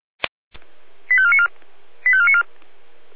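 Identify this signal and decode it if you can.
Five-tone selective call (ZVEI), heard as quick runs of five tones, tones of about 70 ms. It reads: 85494 85494
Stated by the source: Five-tone ZVEI selective call received on SDR.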